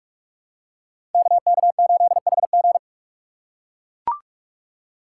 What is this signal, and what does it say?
Morse code, heard as a keyed tone, about 44 wpm, 700 Hz. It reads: KK8HG
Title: Code Groups - 5 Letter:Numbers - 45wpm